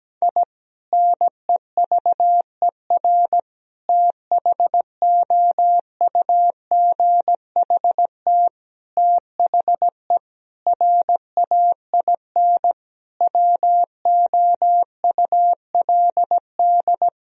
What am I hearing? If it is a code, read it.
Morse code, heard as a keyed tone, about 17 wpm, 700 Hz. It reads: I NEVER THOUGHT THE RAIN WOULD
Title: Sentences from Top 1000 Words - Review Part 1 - 17wpm